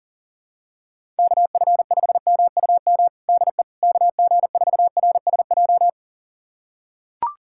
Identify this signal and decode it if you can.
Morse code, heard as a keyed tone, about 40 wpm, 700 Hz. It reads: KF5MUM DE KG4RSJ
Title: USA Callsign DE Callsign - 6 Characters - 40wpm